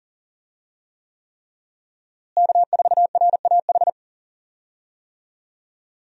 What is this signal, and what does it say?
Morse code, heard as a keyed tone, about 40 wpm, 700 Hz. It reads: K4RAH
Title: USA Call Signs - 5 Characters - Repeated 3x - 40wpm